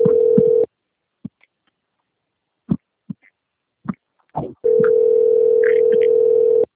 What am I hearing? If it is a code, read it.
Ringback tone (North America), 440 + 480 Hz.
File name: rg-1600-unknown- - - .227